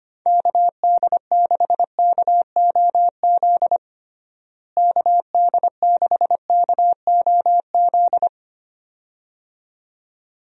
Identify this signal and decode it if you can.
Morse code, heard as a keyed tone, about 25 wpm, 700 Hz. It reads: KD6KOZ KD6KOZ
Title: USA Call Signs - 6 Characters - Repeated 2x - 25wpm